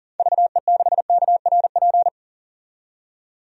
Morse code, heard as a keyed tone, about 40 wpm, 700 Hz. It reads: VE6KRP
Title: DXCC-100 Call Signs - 6 Characters - 40wpm